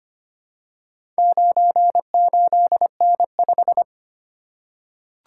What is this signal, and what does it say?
Morse code, heard as a keyed tone, about 25 wpm, 700 Hz. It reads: 98N5
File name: practice / 25wpm / Four Letter-Number Combinations - 25wpm